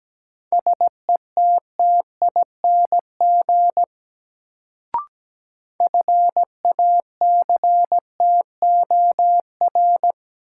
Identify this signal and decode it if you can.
Morse code, heard as a keyed tone, about 17 wpm, 700 Hz. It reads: SETTING FACTOR
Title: Sentences from Top 1300 Words - New Word Drill Alt - 17wpm